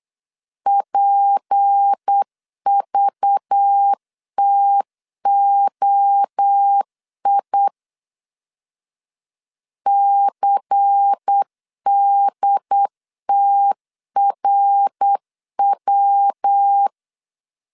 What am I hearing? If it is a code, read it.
Morse code, heard as a keyed tone, about 8 wpm, 800 Hz. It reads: PVTOI CDTRW
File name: LEZIONI / 6 PKX0 / 6 P 8wpm 3025